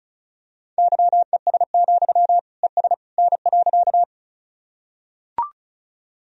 Morse code, heard as a keyed tone, about 34 wpm, 700 Hz. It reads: YES, ES N.
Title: Sentences from Top 500 Words and Abbreviations 35wpm